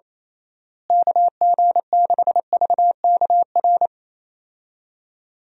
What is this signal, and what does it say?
Morse code, heard as a keyed tone, about 28 wpm, 700 Hz. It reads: KG6VKR
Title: USA Call Signs - 6 Characters - Repeated 3x - 28wpm